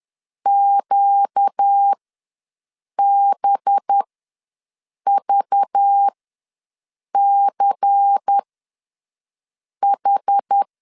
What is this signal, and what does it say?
Morse code, heard as a keyed tone, about 10 wpm, spaced as for 6 wpm, 800 Hz. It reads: QBVCH
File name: LEZIONI / 9 2867 ws 10wpm 4215